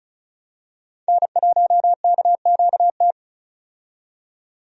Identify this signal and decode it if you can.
Morse code, heard as a keyed tone, about 36 wpm, 700 Hz. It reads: N1KQT
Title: USA Call Signs - 5 Characters - 35wpm